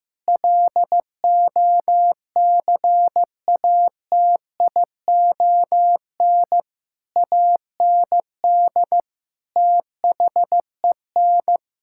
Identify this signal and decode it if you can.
Morse code, heard as a keyed tone, about 15 wpm, 700 Hz. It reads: LOCATION AND THEN
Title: Sentences from Top 1000 Words - Review Part 6 - 15wpm